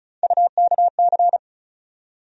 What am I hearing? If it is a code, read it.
Morse code, heard as a keyed tone, about 36 wpm, 700 Hz. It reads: UKC
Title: USA Call Sign Postfixes - 3 Letters 35wpm